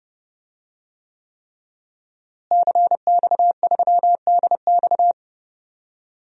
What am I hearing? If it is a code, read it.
Morse code, heard as a keyed tone, about 30 wpm, 700 Hz. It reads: CX3DX
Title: DXCC-100 Call Signs - 5 Characters - Repeated 3x - 30wpm